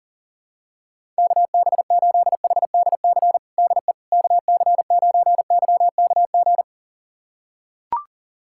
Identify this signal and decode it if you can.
Morse code, heard as a keyed tone, about 37 wpm, 700 Hz. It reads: KB8HDC DE KC9YKG
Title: USA Callsign DE Callsign - 6 Characters - 40wpm